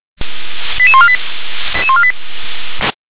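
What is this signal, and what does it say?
Five-tone selective call (ZVEI), heard as quick runs of five tones, tones of about 70 ms. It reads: 88148
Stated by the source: ZVEI five-tone call received by SDR.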